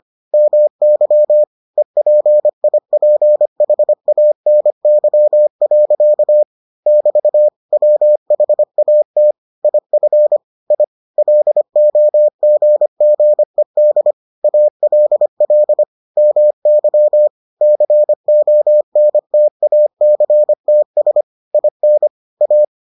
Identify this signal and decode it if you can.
Morse code, heard as a keyed tone, about 25 wpm, 600 Hz. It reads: MY EPIPHANY. <BT> WHAT IF I LOGGED ALL MY CONTACTS IN A